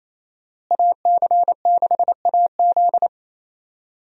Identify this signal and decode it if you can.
Morse code, heard as a keyed tone, about 28 wpm, 700 Hz. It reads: AC6AZ
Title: USA Call Signs - 5 Characters - Repeated 3x - 28wpm